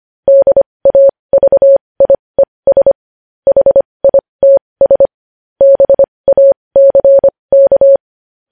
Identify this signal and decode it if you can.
Morse code, heard as a keyed tone, about 25 wpm, spaced as for 20 wpm, 550 Hz. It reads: DAVIES HITS BACK
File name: podcast / News headlines-1x-25x20